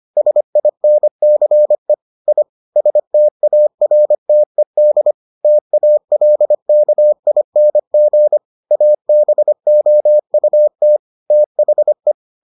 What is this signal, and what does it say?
Morse code, heard as a keyed tone, about 25 wpm, 600 Hz. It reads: SINCE I STARTED TALKING ABOUT THE